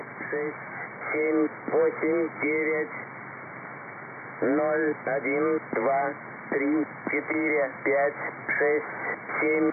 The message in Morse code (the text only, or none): none